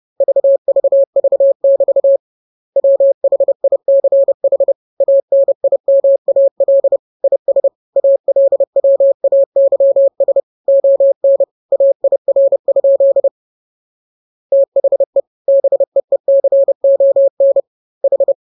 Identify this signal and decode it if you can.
Morse code, heard as a keyed tone, about 30 wpm, 550 Hz. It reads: VVV<BT> WHICH ANIMAL IS ALWAYS ON AIR? THE BEECON H
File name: jokes / joke 28 30 0000